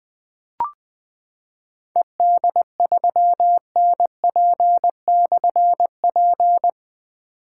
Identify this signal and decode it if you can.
Morse code, heard as a keyed tone, about 20 wpm, 700 Hz. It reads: ED3NP/P
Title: Difficult Call Signs 20wpm